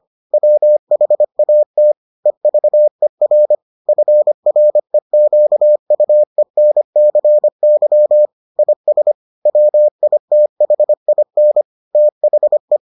Morse code, heard as a keyed tone, about 25 wpm, 600 Hz. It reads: WHAT EVER FREQUENCY IS WITHIN THE